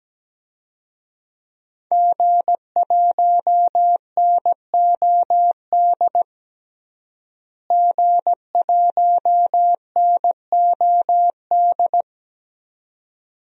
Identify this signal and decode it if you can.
Morse code, heard as a keyed tone, about 17 wpm, 700 Hz. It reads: G1NOD G1NOD
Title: DXCC-100 Call Signs - 5 Characters - Repeated 2x - 17wpm